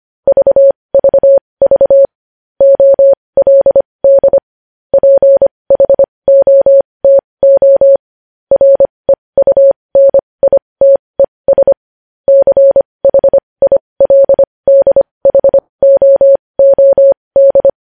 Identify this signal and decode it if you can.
Morse code, heard as a keyed tone, about 25 wpm, spaced as for 20 wpm, 550 Hz. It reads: VVV OLD PHOTO REUNITES CHILDHOOD
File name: podcast / News headlines-1x-25x20